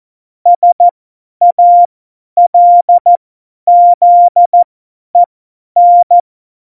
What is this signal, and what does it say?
Morse code, heard as a keyed tone, about 14 wpm, spaced as for 10 wpm, 700 Hz. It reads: SALZEN
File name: Morsekurs / Witze / Witz 03. Oktober 17, Tempo 50 F70